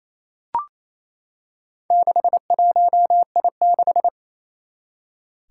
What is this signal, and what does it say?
Morse code, heard as a keyed tone, about 28 wpm, 700 Hz. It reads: 61I6
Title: Four Challenging Letter-Number Combinations 28wpm